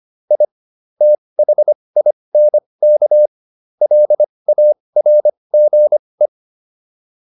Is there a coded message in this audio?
Yes: Morse code from 0.3 to 6.3 s, heard as a keyed tone, about 25 wpm, spaced as for 20 wpm, 600 Hz. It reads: I THINK LARGE